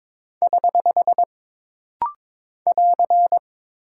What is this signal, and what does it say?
Morse code, heard as a keyed tone, about 22 wpm, 700 Hz. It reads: <HH> <AR>
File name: practice / 22wpm / Single Punctuation-Prosign - Rapid-Fire - 22wpm